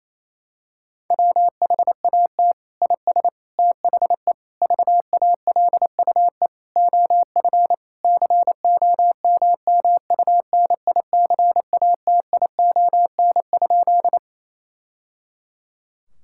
Morse code, heard as a keyed tone, about 27 wpm, 700 Hz. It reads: WHAT IS THE VALUE OF COMMUNICATION?
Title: Sentences from Top 1000 Words - Review Part 6 - 28wpm